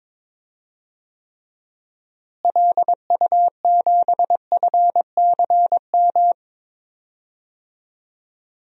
Morse code, heard as a keyed tone, about 21 wpm, 700 Hz. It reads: LU7FCM
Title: DXCC-100 Call Signs - 6 Characters - Repeated 2x - 22wpm